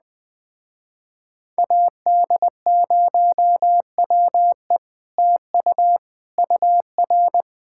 Morse code, heard as a keyed tone, about 20 wpm, 700 Hz. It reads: AD0WE TU UR
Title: Speed-Racing Common QSO Phrases Encore - 17wpm